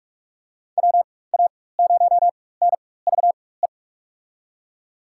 Morse code, heard as a keyed tone, about 46 wpm, spaced as for 21 wpm, 700 Hz. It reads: WA0NVE
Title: USA Call Signs - 4x Character Spacing - 45wpm